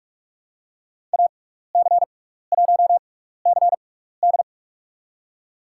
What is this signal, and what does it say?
Morse code, heard as a keyed tone, about 42 wpm, spaced as for 15 wpm, 700 Hz. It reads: AC1CD
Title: USA Call Signs - 6x Character Spacing - 45wpm